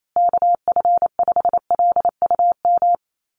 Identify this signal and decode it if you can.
Morse code, heard as a keyed tone, about 28 wpm, 700 Hz. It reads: KF5LUM
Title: USA Call Signs Repeated 2x 28wpm